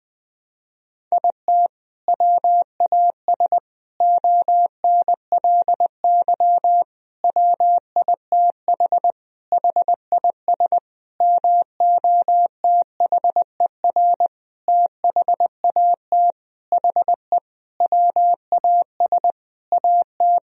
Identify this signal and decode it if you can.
Morse code, heard as a keyed tone, about 20 wpm, 700 Hz. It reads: IT WAS ONLY WITH HIS MOTHER THAT HE WAS AT
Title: Sentences from Top 1000 Words - Review Part 1 - 20wpm